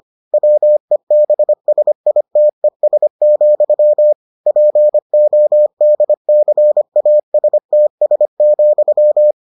Morse code, heard as a keyed tone, about 25 wpm, 600 Hz. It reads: WEBSITES, PODCASTS,